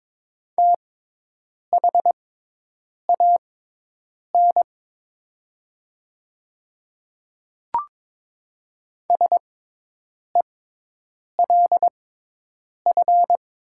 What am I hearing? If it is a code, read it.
Morse code, heard as a keyed tone, about 22 wpm, spaced as for 8 wpm, 700 Hz. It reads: THAN SELF
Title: 6x Character Spacing - Top 300 Words - 22wpm